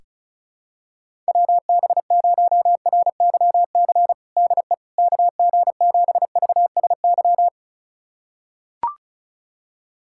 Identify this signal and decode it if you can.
Morse code, heard as a keyed tone, about 36 wpm, 700 Hz. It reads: WB0RYC DE KG7VSY
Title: USA Callsign DE Callsign 35wpm